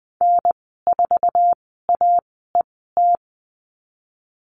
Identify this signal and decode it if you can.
Morse code, heard as a keyed tone, about 20 wpm, spaced as for 15 wpm, 700 Hz. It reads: N4AET
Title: 147 Lesson 24g - Letter y - Call Signs 20wpm